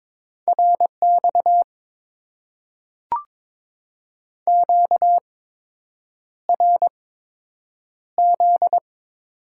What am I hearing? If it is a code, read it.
Morse code, heard as a keyed tone, about 22 wpm, spaced as for 8 wpm, 700 Hz. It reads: RX QRZ
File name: Top 90 QSO Elements - 8x Character Spacing - 22wpm